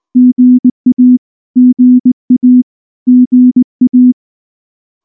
Morse code, heard as a keyed tone, about 18 wpm, 250 Hz. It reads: GA GA GA
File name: morse / morse-training-20WPM-2